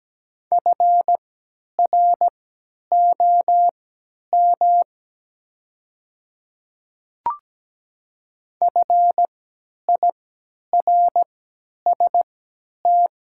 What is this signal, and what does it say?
Morse code, heard as a keyed tone, about 17 wpm, spaced as for 10 wpm, 700 Hz. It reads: FROM FIRST
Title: 3x Character Spacing - Top 100 Words - 17wpm